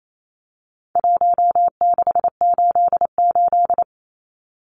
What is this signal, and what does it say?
Morse code, heard as a keyed tone, about 28 wpm, 700 Hz. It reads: 1688
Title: Numbers - Contest Style - 4-Digit 28wpm